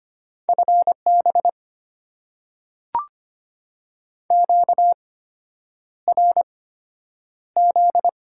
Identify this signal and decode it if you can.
Morse code, heard as a keyed tone, about 25 wpm, spaced as for 9 wpm, 690 Hz. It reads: FB QRZ
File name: Top 90 QSO Elements - 8x Character Spacing - 25wpm